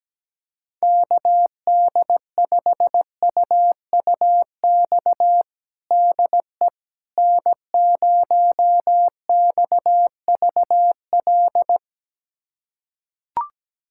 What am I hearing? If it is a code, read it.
Morse code, heard as a keyed tone, about 17 wpm, 700 Hz. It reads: KD5UUX DE N0XVL
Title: USA Callsign DE Callsign 17wpm